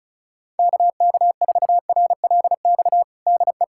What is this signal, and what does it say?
Morse code, heard as a keyed tone, about 36 wpm, 700 Hz. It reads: KK4RLX DE
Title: USA Callsign DE Callsign - 6 Characters - 35wpm